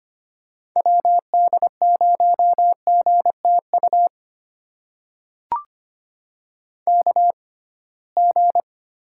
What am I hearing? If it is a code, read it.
Morse code, heard as a keyed tone, about 25 wpm, spaced as for 19 wpm, 700 Hz. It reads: WD0GTU KG
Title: USA Call Signs - 6 Characters - 6x Character Spacing - 25wpm